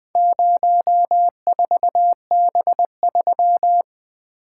Morse code, heard as a keyed tone, about 20 wpm, 700 Hz. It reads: 04B3
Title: Four Letter-Number Combinations - 20wpm